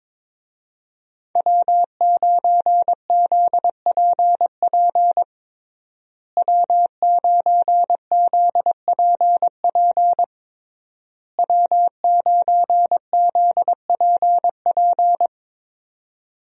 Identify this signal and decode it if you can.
Morse code, heard as a keyed tone, about 22 wpm, 700 Hz. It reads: W9ZPP W9ZPP W9ZPP
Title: USA Call Signs - 5 Characters - Repeated 3x - 22wpm